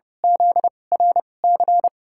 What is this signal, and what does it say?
Morse code, heard as a keyed tone, about 30 wpm, spaced as for 22 wpm, 700 Hz. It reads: ZRC
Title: USA Call Signs - 6 Characters - 2x Character Spacing - 30wpm